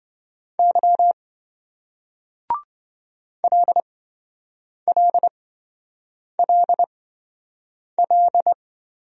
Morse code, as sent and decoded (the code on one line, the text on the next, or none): -.-- / .-.. .-.. .-.. .-..
Y LLLL